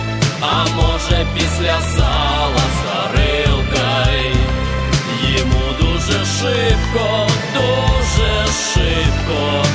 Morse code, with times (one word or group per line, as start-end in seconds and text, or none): none